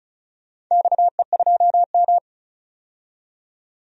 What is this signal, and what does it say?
Morse code, heard as a keyed tone, about 34 wpm, 700 Hz. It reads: XE2M
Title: DXCC-100 Call Signs - 4 Characters - 35wpm